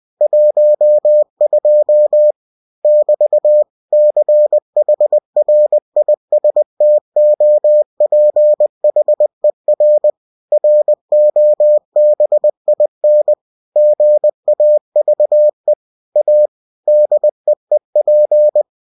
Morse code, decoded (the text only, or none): 12 <BT> CHRISTOPHER ROBIN GAVE A DEEP